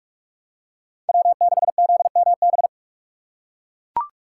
Morse code, heard as a keyed tone, about 41 wpm, 700 Hz. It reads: W6ZMB